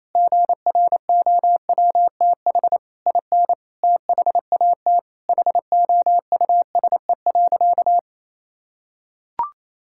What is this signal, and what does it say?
Morse code, heard as a keyed tone, about 28 wpm, 700 Hz. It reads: GROWTH IN THAT HOUSE.
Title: Sentences from Top 1000 Words - Review Part 5 - 28wpm